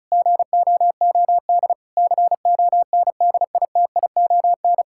Morse code, heard as a keyed tone, about 36 wpm, 700 Hz. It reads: GOOD CONDITION
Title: Pangram Challenge v2 - 35wpm